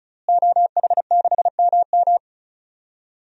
Morse code, heard as a keyed tone, about 34 wpm, 700 Hz. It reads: OH6MM
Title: DXCC-100 Call Signs - 5 Characters - Repeated 3x - 35wpm